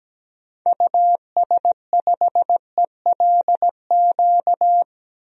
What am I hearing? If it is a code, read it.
Morse code, heard as a keyed tone, about 17 wpm, 700 Hz. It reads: US5ELQ
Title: DXCC-100 Call Signs - 6 Characters - Repeated 2x - 17wpm